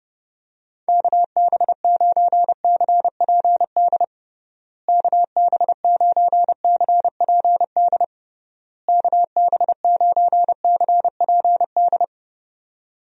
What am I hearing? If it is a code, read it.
Morse code, heard as a keyed tone, about 30 wpm, 700 Hz. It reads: KB9CPD KB9CPD KB9CPD
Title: USA Call Signs - 6 Characters - Repeated 3x - 30wpm